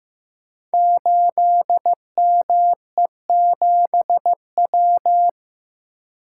Morse code, heard as a keyed tone, about 15 wpm, 700 Hz. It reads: 8ME7W
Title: Code Groups - 5 Letter:Numbers - 15wpm